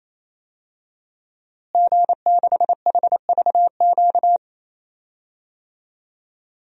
Morse code, heard as a keyed tone, about 28 wpm, 700 Hz. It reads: G6HVQ
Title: DXCC-100 Call Signs - 5 Characters - Repeated 3x - 28wpm